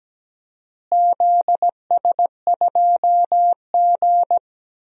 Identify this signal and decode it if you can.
Morse code, heard as a keyed tone, about 17 wpm, 700 Hz. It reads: ZS2G
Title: DXCC-100 Call Signs - 4 Characters - Repeated 2x - 17wpm